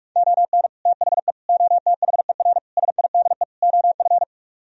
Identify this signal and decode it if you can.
Morse code, heard as a keyed tone, about 38 wpm, 700 Hz. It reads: ON THE OTHER SIDE OF